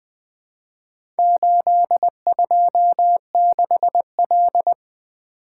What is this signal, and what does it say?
Morse code, heard as a keyed tone, about 20 wpm, 700 Hz. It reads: 826L
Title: Four Letter-Number Combinations - 20wpm